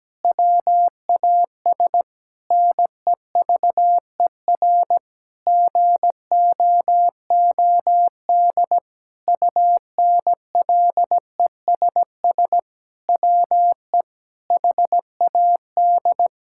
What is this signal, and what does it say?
Morse code, heard as a keyed tone, about 17 wpm, 700 Hz. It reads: WAS NEVER GOOD UNLESS WE HAD
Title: Sentences from Top 1000 Words - Review Part 6 - 17wpm